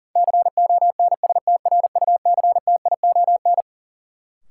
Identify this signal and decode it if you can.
Morse code, heard as a keyed tone, about 40 wpm, 700 Hz. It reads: CONSTRUCTION
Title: Sentences from Top 1600 Words - New Word Drill - 40wpm